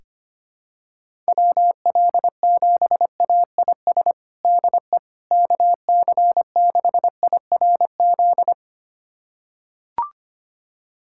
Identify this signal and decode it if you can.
Morse code, heard as a keyed tone, about 25 wpm, 700 Hz. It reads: WL7AIS DE KC6IRZ
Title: USA Callsign DE Callsign 25wpm